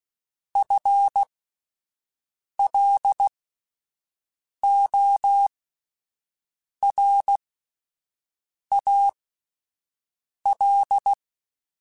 Morse code, heard as a keyed tone, about 16 wpm, spaced as for 5 wpm, 800 Hz. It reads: FLORAL
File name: Spaced / FL Runs sp mp3s / FL Runs sp @10